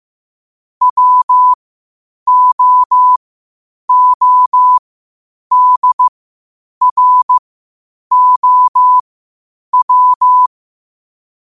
Morse code, heard as a keyed tone, about 15 wpm, spaced as for 8 wpm, 1000 Hz. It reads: WOODROW
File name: qotdfiles / 8wpm